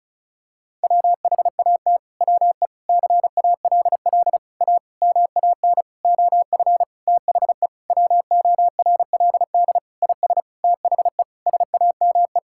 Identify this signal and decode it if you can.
Morse code, heard as a keyed tone, about 36 wpm, 700 Hz. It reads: WHAT WE CALL A MAN OF THE WORLD IS THE SAME
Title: Sentences from Top 1600 Words 35wpm